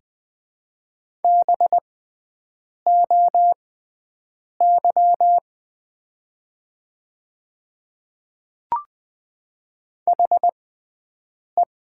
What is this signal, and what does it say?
Morse code, heard as a keyed tone, about 20 wpm, spaced as for 7 wpm, 700 Hz. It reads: BOY HE
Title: 6x Character Spacing - Top 200 Words - 20wpm